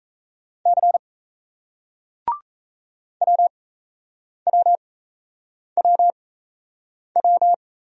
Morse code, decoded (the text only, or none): C WWWW